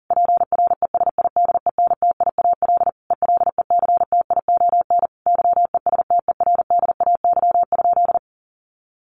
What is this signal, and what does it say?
Morse code, heard as a keyed tone, about 40 wpm, 700 Hz. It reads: PRESIDENTIAL ELECTION YESTERDAY?
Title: A Sound Of Thunder 40wpm